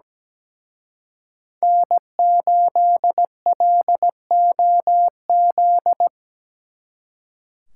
Morse code, heard as a keyed tone, about 20 wpm, 700 Hz. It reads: N8LOZ
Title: Speed Racing - USA 1x3 Call Signs 17wpm